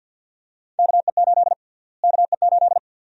Morse code, heard as a keyed tone, about 47 wpm, 700 Hz. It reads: KE8 KE8
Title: USA Call Sign Prefixes - Repeated 2x - 50wpm